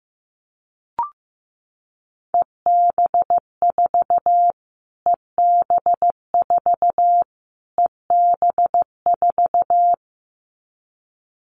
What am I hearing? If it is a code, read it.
Morse code, heard as a keyed tone, about 15 wpm, 700 Hz. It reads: EB4 EB4 EB4
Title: Callsign Prefixes for Top 25 DXCC - Repeated 3x - 15wpm